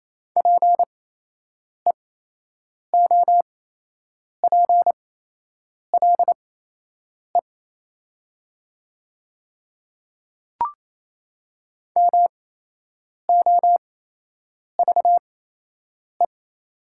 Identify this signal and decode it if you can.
Morse code, heard as a keyed tone, about 28 wpm, spaced as for 8 wpm, 700 Hz. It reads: PEOPLE MOVE
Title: 8x Character Spacing - Top 300 Words - 28wpm